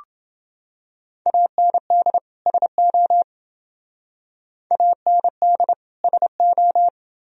Morse code, heard as a keyed tone, about 28 wpm, 700 Hz. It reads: AND SO AND SO